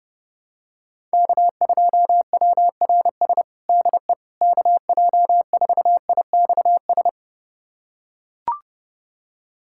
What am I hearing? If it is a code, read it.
Morse code, heard as a keyed tone, about 30 wpm, 700 Hz. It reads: K2WRS DE KJ4IXS
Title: USA Callsign DE Callsign 30wpm